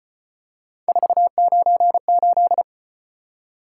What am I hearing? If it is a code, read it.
Morse code, heard as a keyed tone, about 33 wpm, 700 Hz. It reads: 498